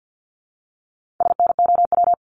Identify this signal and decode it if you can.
Morse code, heard as a keyed tone, about 50 wpm, 700 Hz. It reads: SNOW
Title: Multi-letter Phonemes Repeated 3x 50wpm